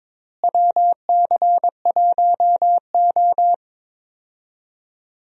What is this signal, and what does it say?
Morse code, heard as a keyed tone, about 22 wpm, 700 Hz. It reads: WC1O
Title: USA Call Signs - 4 Characters - Repeated 2x - 22wpm